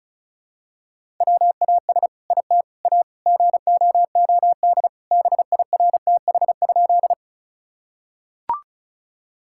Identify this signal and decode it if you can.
Morse code, heard as a keyed tone, about 34 wpm, 700 Hz. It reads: WAS IT A GOOD BIRTH?
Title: Sentences from Top 1000 Words - Review Part 5 - 35wpm